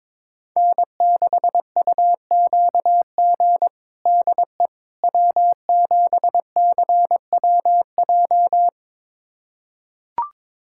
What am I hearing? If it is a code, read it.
Morse code, heard as a keyed tone, about 22 wpm, 700 Hz. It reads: N6UQG DE W7CWJ
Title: USA Callsign DE Callsign - 5 Characters - 22wpm